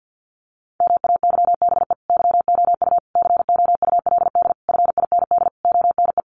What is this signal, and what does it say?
Morse code, heard as a keyed tone, about 50 wpm, 700 Hz. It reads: MAYBE YOU COULD FIND ONE
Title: Sentences from Top 1400 Words 50wpm